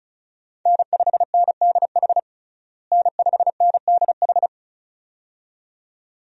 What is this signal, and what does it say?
Morse code, heard as a keyed tone, about 36 wpm, 700 Hz. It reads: N5NDH N5NDH
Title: USA Call Signs - 5 Characters - Repeated 2x - 35wpm